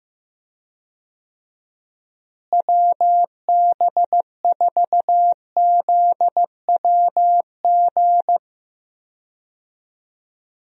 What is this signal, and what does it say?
Morse code, heard as a keyed tone, about 15 wpm, 700 Hz. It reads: WB4ZWG
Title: USA Call Signs - 6 Characters - Repeated 3x - 15wpm